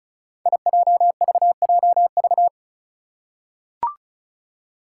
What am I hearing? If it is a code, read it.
Morse code, heard as a keyed tone, about 36 wpm, 700 Hz. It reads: IJVJV
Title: Five Challenging Letter-Number Combinations 35wpm